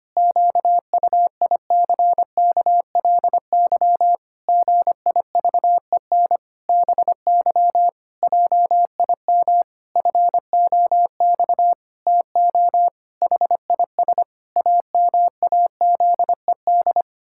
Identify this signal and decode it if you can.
Morse code, heard as a keyed tone, about 25 wpm, 700 Hz. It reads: QUICKLY GIVEN BY JIM FOX TO HIS AMAZED
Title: Pangram Challenge v2 - 25wpm